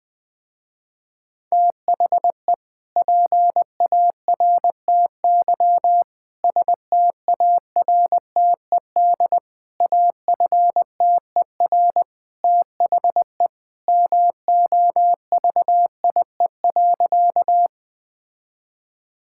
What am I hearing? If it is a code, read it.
Morse code, heard as a keyed tone, about 20 wpm, 700 Hz. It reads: THE PARTY STARTED AFTER THE MOVIE.
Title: Sentences with 2-Word Phrases - Spoken Beginning - 20wpm